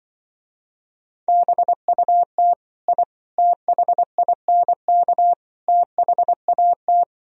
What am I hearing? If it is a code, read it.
Morse code, heard as a keyed tone, about 24 wpm, 700 Hz. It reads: BUT I THINK THAT
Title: Speed-Racing - Sets of 4 Words 20wpm